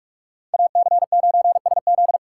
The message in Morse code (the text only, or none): AC9SZ